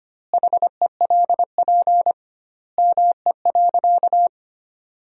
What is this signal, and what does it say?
Morse code, heard as a keyed tone, about 25 wpm, 700 Hz. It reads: HELP ME.
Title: 2x Word-Spacing - Sentences from Top 200 Words - 25wpm